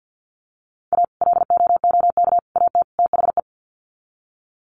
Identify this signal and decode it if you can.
Morse code, heard as a keyed tone, about 48 wpm, 700 Hz. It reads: A LOOK AT THE
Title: Sets of 4 Words 50wpm